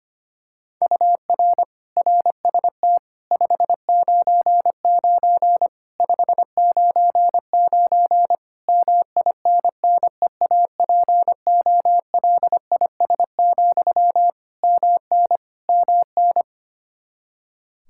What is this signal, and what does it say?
Morse code, heard as a keyed tone, about 25 wpm, 700 Hz. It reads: UR RST 599 599 MINNEAPOLIS, MN MN
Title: Practice QSO - Exchange 1 - Beginning - 25wpm